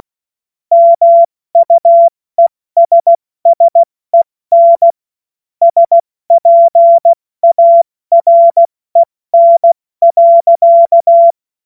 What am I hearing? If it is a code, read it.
Morse code, heard as a keyed tone, about 16 wpm, 680 Hz. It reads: MUESSEN SPAREN.
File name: Witz-08-Juli-2025-70-F80